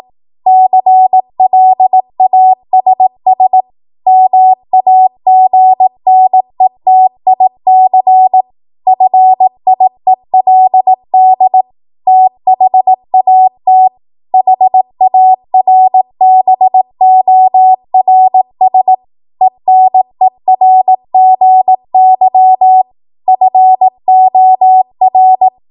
Morse code, heard as a keyed tone, about 18 wpm, 750 Hz. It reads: CLASS MAGNETIC FIELD THAT HARBORS ENERGY FOR
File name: Morse / 18WPM